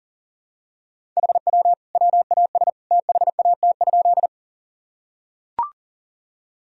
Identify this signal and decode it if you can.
Morse code, heard as a keyed tone, about 37 wpm, 700 Hz. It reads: HW WAS THAT?
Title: Sentences from Top 100 Words and Abbreviations 40wpm